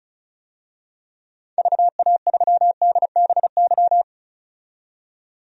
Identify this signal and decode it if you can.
Morse code, heard as a keyed tone, about 34 wpm, 700 Hz. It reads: VA3DBY
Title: DXCC-100 Call Signs - 6 Characters - 35wpm